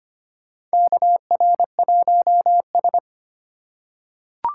Morse code, heard as a keyed tone, about 25 wpm, 700 Hz. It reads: KR1S
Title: USA Call Signs - 2x Character Spacing - 25wpm